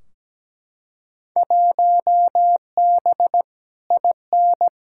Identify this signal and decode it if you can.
Morse code, heard as a keyed tone, about 17 wpm, 700 Hz. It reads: 1B IN
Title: Field Day Exchanges - 17wpm